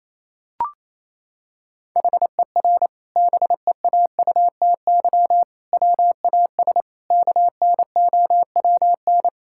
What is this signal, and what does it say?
Morse code, heard as a keyed tone, about 28 wpm, 700 Hz. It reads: HER BEAUTY WAS KNOWN
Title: Sentences from Top 1000 Words - Review Part 6 - 28wpm